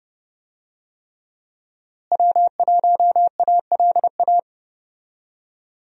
Morse code, heard as a keyed tone, about 30 wpm, 700 Hz. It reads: W1ALA
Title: USA Call Signs - 5 Characters - Repeated 3x - 30wpm